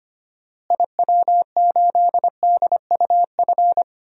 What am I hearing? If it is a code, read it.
Morse code, heard as a keyed tone, about 25 wpm, 700 Hz. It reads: IW8DUF
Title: DXCC-100 Call Signs - 6 Characters - Repeated 2x - 25wpm